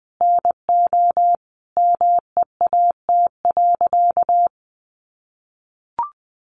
Morse code, heard as a keyed tone, about 20 wpm, 700 Hz. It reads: NO MEAT.
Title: Sentences from Top 1300 Words 20wpm